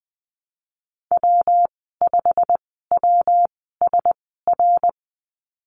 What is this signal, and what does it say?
Morse code, heard as a keyed tone, about 20 wpm, spaced as for 14 wpm, 700 Hz. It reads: W5WSR
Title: 128 Lesson 21g - Number 6 - Call Signs 20wpm